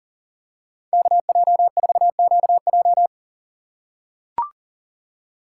Morse code, heard as a keyed tone, about 38 wpm, 700 Hz. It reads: KJ4QJ
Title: USA Call Signs - 6x Character Spacing - 40wpm